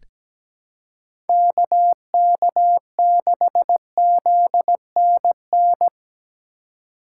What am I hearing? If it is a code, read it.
Morse code, heard as a keyed tone, about 17 wpm, 700 Hz. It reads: KK6ZNN
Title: USA Call Signs - 6 Characters - 2x Character Spacing - 17wpm